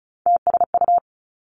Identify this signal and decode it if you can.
Morse code, heard as a keyed tone, about 36 wpm, 700 Hz. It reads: TSU
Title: Three Challenging Letter-Number Combinations 35wpm